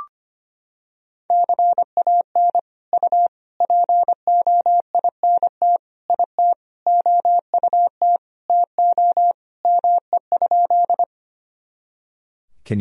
Morse code, heard as a keyed tone, about 25 wpm, 700 Hz. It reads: CAN U POINT IT OUT TO ME?
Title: Sentences from Top 200 Words and Abbreviations 25wpm